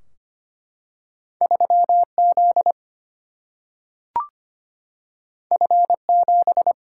Morse code, heard as a keyed tone, about 25 wpm, 700 Hz. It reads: 3 Z F 7
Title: Single-Letter Single-Number 25wpm